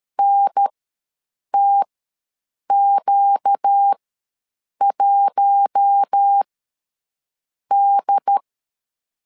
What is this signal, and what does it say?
Morse code, heard as a keyed tone, about 13 wpm, spaced as for 7 wpm, 800 Hz. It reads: NTQ1D
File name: LEZIONI / 4 Q1 ws 12wpm 2129